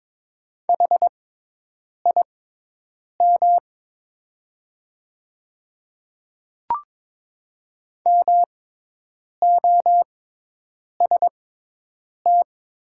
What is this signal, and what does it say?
Morse code, heard as a keyed tone, about 22 wpm, spaced as for 8 wpm, 700 Hz. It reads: HIM MOST
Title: 6x Character Spacing - Top 100 Words - 22wpm